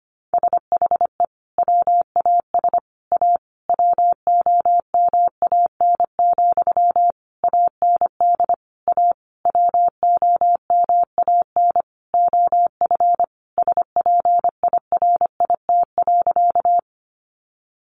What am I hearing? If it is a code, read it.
Morse code, heard as a keyed tone, about 25 wpm, 700 Hz. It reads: SHE WAS A WOMAN, AND A WOMAN OF SPIRIT.
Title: Sentences from Top 1400 Words 25wpm